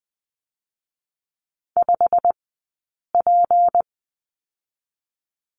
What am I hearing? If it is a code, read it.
Morse code, heard as a keyed tone, about 20 wpm, spaced as for 8 wpm, 700 Hz. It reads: 5P
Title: 197 Lesson 32d - Letter k - Two Characters 20wpm